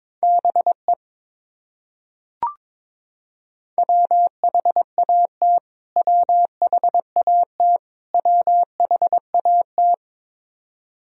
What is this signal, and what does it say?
Morse code, heard as a keyed tone, about 22 wpm, 700 Hz. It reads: BE WHAT WHAT WHAT
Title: Top 100 Words Repeated 22wpm